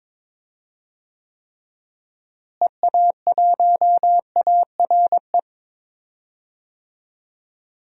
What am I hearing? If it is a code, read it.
Morse code, heard as a keyed tone, about 22 wpm, 700 Hz. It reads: EA1ARE
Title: DXCC-100 Call Signs - 6 Characters - Repeated 2x - 22wpm